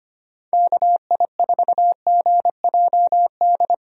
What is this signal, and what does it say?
Morse code, heard as a keyed tone, about 25 wpm, 700 Hz. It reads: KI4GJD
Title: USA Call Signs - 6 Characters - 25wpm